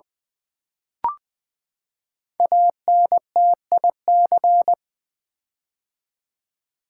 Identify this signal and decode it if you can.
Morse code, heard as a keyed tone, about 20 wpm, 700 Hz. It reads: ANTIC
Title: Build A Word - 5 Letters At A Time - 10-Letter Words - 20wpm